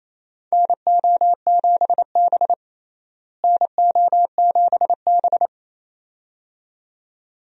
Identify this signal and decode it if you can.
Morse code, heard as a keyed tone, about 28 wpm, 700 Hz. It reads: NO7B NO7B
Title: USA Call Signs - 4 Characters - Repeated 2x - 28wpm